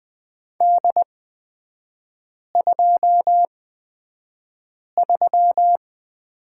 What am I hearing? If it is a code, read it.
Morse code, heard as a keyed tone, about 20 wpm, spaced as for 5 wpm, 700 Hz. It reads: D23
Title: 243 Lesson 39e - Letter z - Three Characters 20wpm